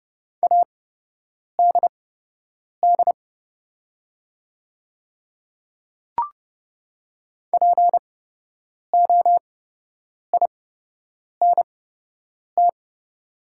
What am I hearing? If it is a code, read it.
Morse code, heard as a keyed tone, about 31 wpm, spaced as for 8 wpm, 690 Hz. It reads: ADD POINT
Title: 8x Character Spacing - Top 200 Words - 30wpm